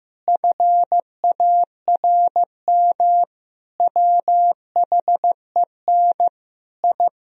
Morse code, heard as a keyed tone, about 15 wpm, 700 Hz. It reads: FARM WHEN I
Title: Sentences from Top 1000 Words - Review Part 2 - 15wpm